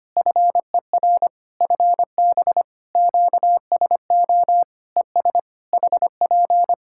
Morse code, heard as a keyed tone, about 25 wpm, 700 Hz. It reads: FER FB QSO ES HP
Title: Practice QSO - Exchange 4 - Ending - 25wpm